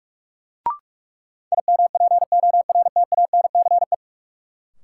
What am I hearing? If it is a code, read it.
Morse code, heard as a keyed tone, about 46 wpm, 700 Hz. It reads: IMPORTANCE
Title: Sentences from Top 1200 Words - New Word Drill - 45wpm